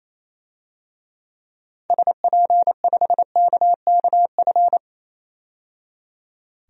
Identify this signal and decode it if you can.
Morse code, heard as a keyed tone, about 28 wpm, 700 Hz. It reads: SP5KKF
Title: DXCC-100 Call Signs - 6 Characters - 28wpm